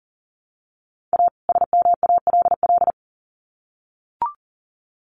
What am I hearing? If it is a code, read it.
Morse code, heard as a keyed tone, about 40 wpm, 700 Hz. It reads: A SMALL
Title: Sets of 2 Words 40wpm